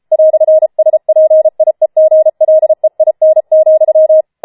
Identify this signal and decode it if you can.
Morse code, heard as a keyed tone, about 33 wpm, 630 Hz. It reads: "SPIEGLEIN,
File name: Schneewittchen Teil5 34wpm